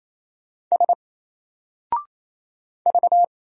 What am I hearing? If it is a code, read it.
Morse code, heard as a keyed tone, about 27 wpm, 700 Hz. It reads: S V
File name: Single-Letter-Number-28wpm